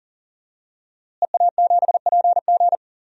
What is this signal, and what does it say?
Morse code, heard as a keyed tone, about 37 wpm, 700 Hz. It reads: EA7PG
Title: DXCC-100 Call Signs - 5 Characters - Repeated 3x - 40wpm